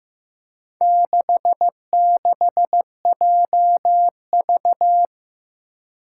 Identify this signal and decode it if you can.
Morse code, heard as a keyed tone, about 16 wpm, 700 Hz. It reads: 66JV
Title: Four Challenging Letter-Number Combinations 15wpm